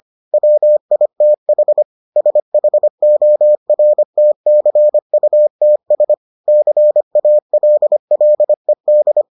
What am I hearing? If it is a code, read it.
Morse code, heard as a keyed tone, about 25 wpm, 600 Hz. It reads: WITH SHORTCUTS CALLED